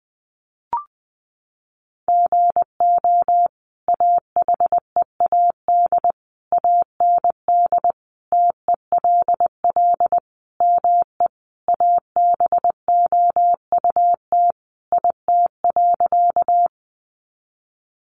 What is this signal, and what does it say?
Morse code, heard as a keyed tone, about 20 wpm, 700 Hz. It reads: GO AHEAD AND TELL ME ABOUT IT.
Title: Sentences from Top 1100 Words 20wpm